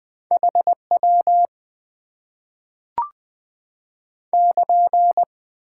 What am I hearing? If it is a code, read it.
Morse code, heard as a keyed tone, about 20 wpm, 700 Hz. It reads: HW <KN>
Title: Top 90 QSO Elements - 6x Character Spacing - 20wpm